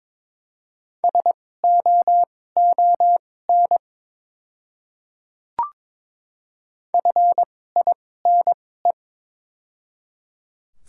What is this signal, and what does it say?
Morse code, heard as a keyed tone, about 22 wpm, spaced as for 16 wpm, 700 Hz. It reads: SOON FINE
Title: 2x Character Spacing - Top 500 Words - 22wpm